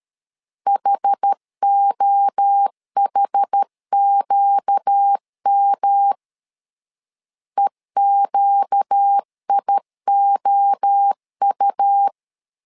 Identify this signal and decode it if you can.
Morse code, heard as a keyed tone, about 13 wpm, 800 Hz. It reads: HOHQM EQIOU
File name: LEZIONI / 4 Q 12wpm 1833